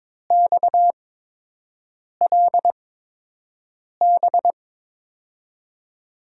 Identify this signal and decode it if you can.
Morse code, heard as a keyed tone, about 22 wpm, spaced as for 6 wpm, 700 Hz. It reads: XLB
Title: USA Call Signs - 5 Characters - 8x Character Spacing - 22wpm